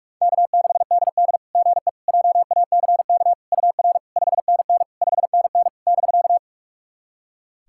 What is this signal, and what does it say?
Morse code, heard as a keyed tone, about 42 wpm, 690 Hz. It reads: K6DD GE JACK UR 5NN 5NN <BK>
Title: SOTA QSO Practice 45wpm